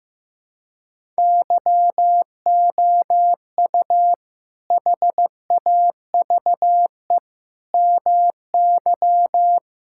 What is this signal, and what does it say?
Morse code, heard as a keyed tone, about 15 wpm, 700 Hz. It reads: YOU HAVE MY
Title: Sentences from Top 1000 Words - Review Part 6 - 15wpm